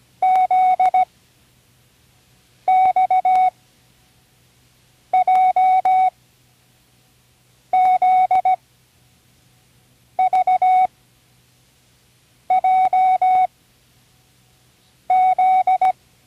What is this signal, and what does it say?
Morse code, heard as a keyed tone, about 17 wpm, 750 Hz. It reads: Z X J Z V J Z
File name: Learning Morse Code (from Tune in the World)/27 random